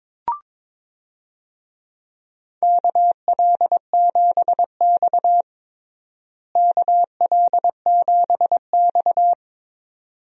Morse code, heard as a keyed tone, about 22 wpm, 700 Hz. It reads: KL7X KL7X
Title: DXCC-100 Call Signs - 4 Characters - Repeated 2x - 22wpm